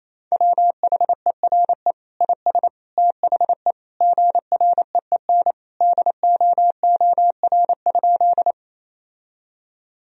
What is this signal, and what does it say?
Morse code, heard as a keyed tone, about 28 wpm, 700 Hz. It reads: WHERE IS THE GREEN DOOR?
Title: Sentences from Top 1000 Words - Review Part 2 - 28wpm